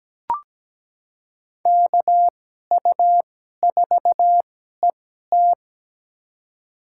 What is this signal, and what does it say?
Morse code, heard as a keyed tone, about 17 wpm, spaced as for 12 wpm, 700 Hz. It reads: KU4ET
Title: USA Call Signs - 2x Character Spacing - 17wpm